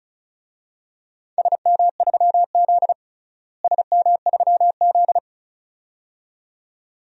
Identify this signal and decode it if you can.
Morse code, heard as a keyed tone, about 36 wpm, 700 Hz. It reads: SM3Z SM3Z
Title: DXCC-100 Call Signs - 4 Characters - Repeated 2x - 35wpm